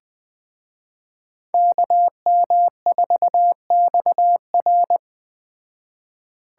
Morse code, heard as a keyed tone, about 20 wpm, 700 Hz. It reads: KM4XR
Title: Speed Racing - USA 2x2 Call Signs 17wpm